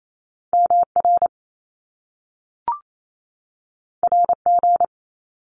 Morse code, heard as a keyed tone, about 28 wpm, 700 Hz. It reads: M R R G
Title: Top Two Letter Combinations in Words 28wpm